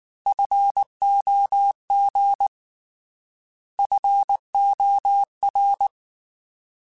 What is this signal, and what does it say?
Morse code, heard as a keyed tone, about 19 wpm, 800 Hz. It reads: FOG FOR
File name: Alphabetical / Alphabetical mp3s / Fs @18